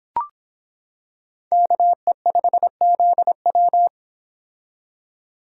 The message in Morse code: KE5ZW

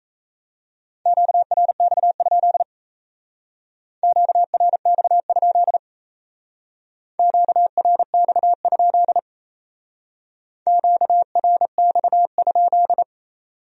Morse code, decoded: QRX? QRX? QRX? QRX?

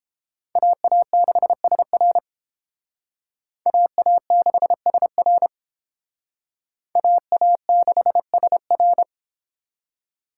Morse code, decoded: AA6SR AA6SR AA6SR

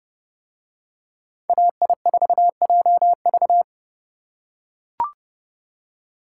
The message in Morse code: AI4JV

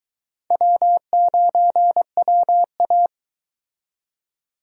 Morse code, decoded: W9WA

